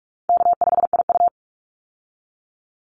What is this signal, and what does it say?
Morse code, heard as a keyed tone, about 44 wpm, 700 Hz. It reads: K5IU
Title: Easy Call Signs Repeated 3x - 45wpm